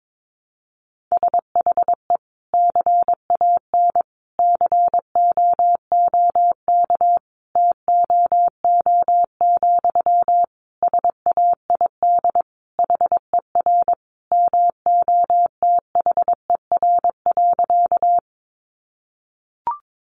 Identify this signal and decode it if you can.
Morse code, heard as a keyed tone, about 22 wpm, 700 Hz. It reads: SHE CAN COOK TOO, SAID HER MOTHER.
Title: Sentences from Top 700 Words 22wpm